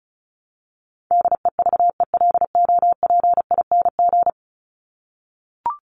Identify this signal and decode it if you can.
Morse code, heard as a keyed tone, about 34 wpm, 700 Hz. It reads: DEVELOPING
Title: ING Suffix - Top 100 Words with Base Words 35wpm